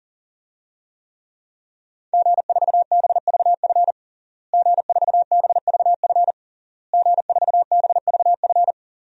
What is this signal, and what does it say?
Morse code, heard as a keyed tone, about 40 wpm, 700 Hz. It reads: G4BVF G4BVF G4BVF
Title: DXCC-100 Call Signs - 5 Characters - Repeated 3x - 40wpm